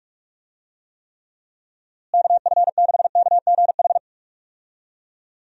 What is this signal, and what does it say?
Morse code, heard as a keyed tone, about 46 wpm, 700 Hz. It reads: KF6KGH
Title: USA Call Signs - 6 Characters - Repeated 3x - 45wpm